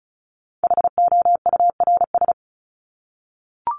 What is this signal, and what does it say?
Morse code, heard as a keyed tone, about 34 wpm, 700 Hz. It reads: HOURS
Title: ES:S Suffix - Top 100 Words with Base Words 35wpm